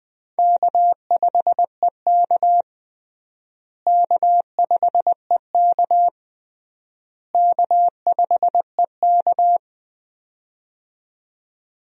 Morse code, heard as a keyed tone, about 20 wpm, 700 Hz. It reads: K5EK K5EK K5EK
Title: USA Call Signs - 4 Characters - Repeated 3x - 20wpm